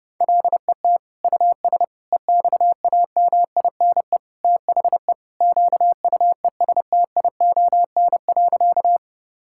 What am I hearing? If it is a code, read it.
Morse code, heard as a keyed tone, about 30 wpm, 700 Hz. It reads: LET US EXAMINE THE QUESTION.
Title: Sentences from Top 1600 Words 30wpm